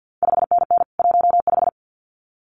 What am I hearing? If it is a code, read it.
Morse code, heard as a keyed tone, about 50 wpm, 700 Hz. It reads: 5NN 15
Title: CQ WW DX Contest Exchange - 50wpm